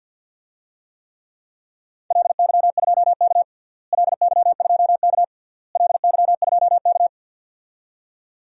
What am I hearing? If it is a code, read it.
Morse code, heard as a keyed tone, about 50 wpm, 700 Hz. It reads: LY2K LY2K LY2K
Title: DXCC-100 Call Signs - 4 Characters - Repeated 3x - 50wpm